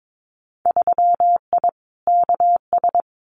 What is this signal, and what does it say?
Morse code, heard as a keyed tone, about 22 wpm, 700 Hz. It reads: 3I KS
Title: Winter Field Day Exchanges 22wpm